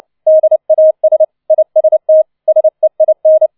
Morse code, heard as a keyed tone, about 29 wpm, 630 Hz. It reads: DAS IST SEIN